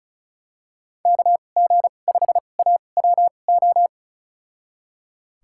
Morse code, heard as a keyed tone, about 36 wpm, spaced as for 25 wpm, 700 Hz. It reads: KG5AWO
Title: USA Call Signs - 6 Characters - 2x Character Spacing - 35wpm